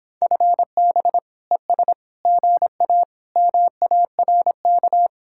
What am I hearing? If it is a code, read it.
Morse code, heard as a keyed tone, about 26 wpm, 700 Hz. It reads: FB ES GA MARK